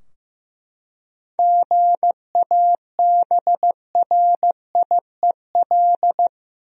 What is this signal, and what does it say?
Morse code, heard as a keyed tone, about 15 wpm, 700 Hz. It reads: GABRIEL
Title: USA Common Names - 15wpm